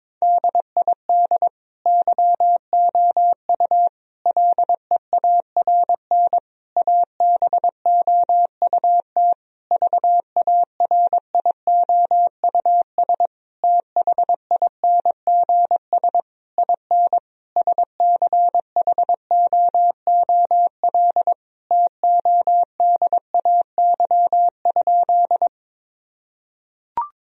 Morse code, heard as a keyed tone, about 22 wpm, 700 Hz. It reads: DID YOU LEARN ABOUT VARIOUS THINGS IN SCHOOL TODAY?
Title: Sentences from Top 1000 Words - Review Part 4 - 22wpm